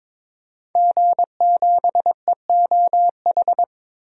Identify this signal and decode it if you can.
Morse code, heard as a keyed tone, about 22 wpm, 700 Hz. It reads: G7EOH
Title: DXCC-100 Call Signs - 5 Characters - 22wpm